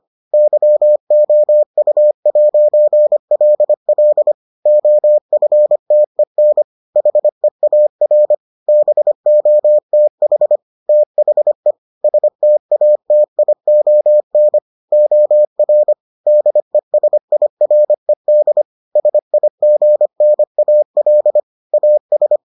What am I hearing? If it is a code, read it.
Morse code, heard as a keyed tone, about 25 wpm, 600 Hz. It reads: YOU'LL OFTEN HEAR BOTH THE STATION OR DESIRED SIGNAL AS